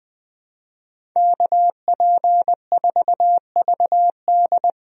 Morse code, heard as a keyed tone, about 20 wpm, 700 Hz. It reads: KP4VD
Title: DXCC-100 Call Signs - 5 Characters - Repeated 3x - 20wpm